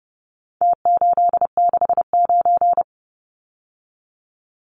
Morse code, heard as a keyed tone, about 30 wpm, 700 Hz. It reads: T869
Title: Numbers - Contest Style - 4-Digit 30wpm